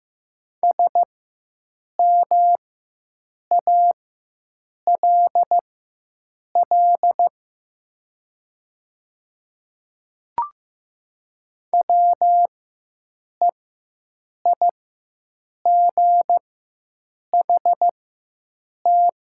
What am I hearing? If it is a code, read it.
Morse code, heard as a keyed tone, about 15 wpm, spaced as for 7 wpm, 700 Hz. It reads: SMALL WEIGHT
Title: 4x Character Spacing - Top 500 Words - 15wpm